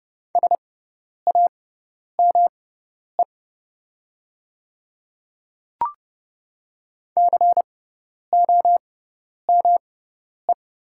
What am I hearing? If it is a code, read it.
Morse code, heard as a keyed tone, about 30 wpm, spaced as for 10 wpm, 700 Hz. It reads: SAME COME
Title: 6x Character Spacing - Top 200 Words - 30wpm